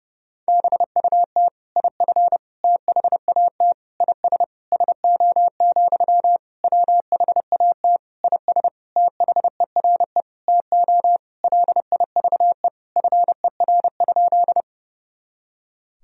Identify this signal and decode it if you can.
Morse code, heard as a keyed tone, about 32 wpm, 700 Hz. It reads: BUT IF THAT IS SO, WHAT IS THERE TO LIVE FER?
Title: Sentences from Top 200 Words and Abbreviations 30wpm